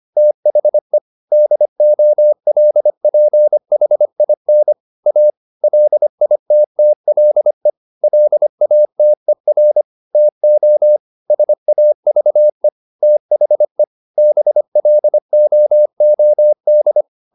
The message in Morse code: THE DOLPHIN A LITTLE LATER TO SAVE THE BLOOD

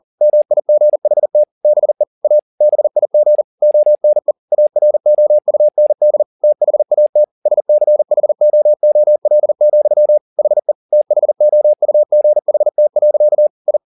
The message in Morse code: MIGHT BE A BIG ONE AROUND THAT SCHOOL, HE THOUGHT. I

